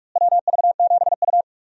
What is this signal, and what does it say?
Morse code, heard as a keyed tone, about 47 wpm, 700 Hz. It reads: WV7U
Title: USA Call Signs - 4 Characters - Repeated 3x - 45wpm